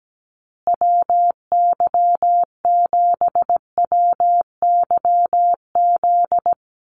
Morse code, heard as a keyed tone, about 17 wpm, 700 Hz. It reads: WY7WYZ
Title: USA Call Signs Repeated 2x 17wpm